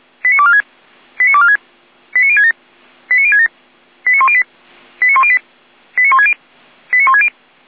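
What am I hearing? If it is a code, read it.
Five-tone selective call (ZVEI), heard as quick runs of five tones, tones of about 70 ms. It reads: 89246 89246 89067 89067 89108 89108 89160 89160
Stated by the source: Five-tone ZVEI selective call 89072, received via SDR.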